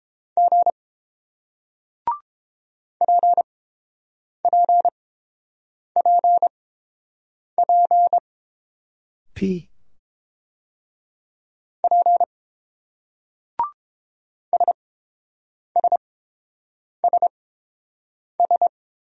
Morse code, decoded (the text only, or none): G PPPP P SSSS